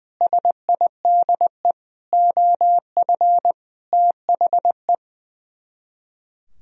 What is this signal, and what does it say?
Morse code, heard as a keyed tone, about 20 wpm, 700 Hz. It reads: SIDE OF THE
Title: Speed-Racing - Sets of 5 Words 20wpm